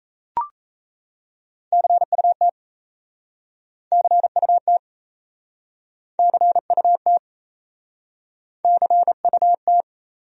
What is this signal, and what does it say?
Morse code, heard as a keyed tone, about 38 wpm, 700 Hz. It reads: CUT CUT CUT CUT